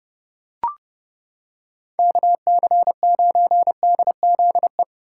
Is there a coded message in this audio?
Yes: Morse code from 2.0 to 4.8 s, heard as a keyed tone, about 29 wpm, 700 Hz. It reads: KC9DZE